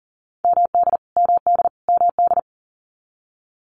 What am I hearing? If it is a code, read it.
Morse code, heard as a keyed tone, about 40 wpm, 700 Hz. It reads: MD MD MD
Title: US States Repeated 3x 40wpm